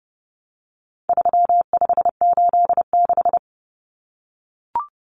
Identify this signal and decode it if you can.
Morse code, heard as a keyed tone, about 30 wpm, 700 Hz. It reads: 3586
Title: Numbers - 4-Digit 30wpm